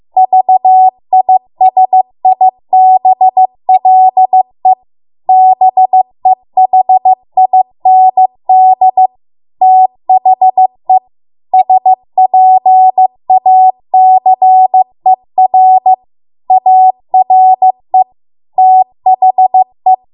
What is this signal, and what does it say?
Morse code, heard as a keyed tone, about 15 wpm, 750 Hz. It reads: VISIBLE BEHIND THE SPACER ARE THE